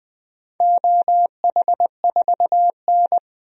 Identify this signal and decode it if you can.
Morse code, heard as a keyed tone, about 21 wpm, 700 Hz. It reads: OH4N
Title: DXCC-100 Call Signs - 4 Characters - 20wpm